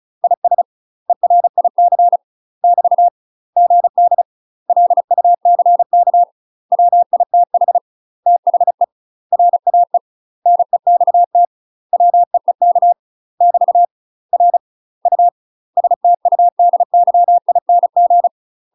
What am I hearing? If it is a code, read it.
Morse code, heard as a keyed tone, about 34 wpm, 700 Hz. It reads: IS ERIC <BT> GD LUCK WITH THE RAE NEXT WEEK <BT> R U STUDYING
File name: UK MASTER QSOs TRAINING / 35WPM / UK MASTER QSOs 0043